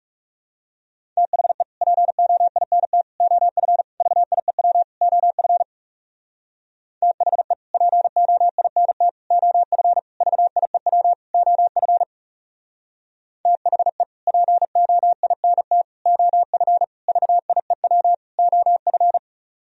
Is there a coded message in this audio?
Yes: Morse code from 1.2 to 19.2 s, heard as a keyed tone, about 39 wpm, 700 Hz. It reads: THE POINT OF VIEW OF THE POINT OF VIEW OF THE POINT OF VIEW OF